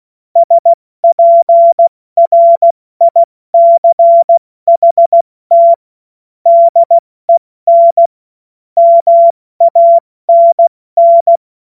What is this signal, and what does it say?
Morse code, heard as a keyed tone, about 16 wpm, 680 Hz. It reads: SPRICHT DEN MANN
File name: Morsekurs / Witze / Witz 13. Januar 2026 70 F80